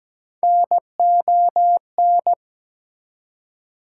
Morse code, heard as a keyed tone, about 17 wpm, 700 Hz. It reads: NON
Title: Prefixes and Suffixes 17wpm